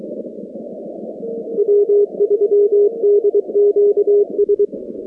Morse code, heard as a keyed tone, about 23 wpm, 410 Hz. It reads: W3DQS
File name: contests / 2019-cw-wpx-kt5j-k5pi- -2330 0-46